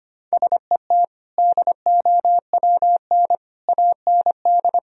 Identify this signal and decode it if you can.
Morse code, heard as a keyed tone, about 25 wpm, 700 Hz. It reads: SET DOWN AND
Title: Sentences from Top 1000 Words - Review Part 1 - 25wpm